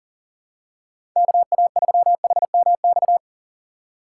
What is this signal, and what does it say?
Morse code, heard as a keyed tone, about 39 wpm, 700 Hz. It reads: KA3HMX
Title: USA Call Signs - 6 Characters - 2x Character Spacing - 40wpm